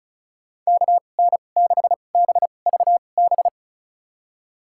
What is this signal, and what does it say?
Morse code, heard as a keyed tone, about 36 wpm, spaced as for 25 wpm, 700 Hz. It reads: KN6BVB
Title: USA Call Signs - 2x Character Spacing - 35wpm